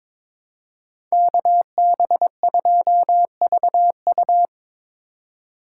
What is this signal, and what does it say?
Morse code, heard as a keyed tone, about 27 wpm, 700 Hz. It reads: KB2VU
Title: Speed Racing - USA 2x2 Call Signs 22wpm